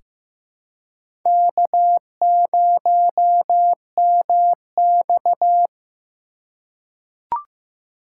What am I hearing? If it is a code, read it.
Morse code, heard as a keyed tone, about 15 wpm, 700 Hz. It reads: K0MX
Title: USA Call Signs - 4 Characters - 2x Character Spacing - 15wpm